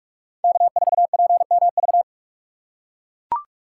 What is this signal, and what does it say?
Morse code, heard as a keyed tone, about 42 wpm, 700 Hz. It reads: K4PMV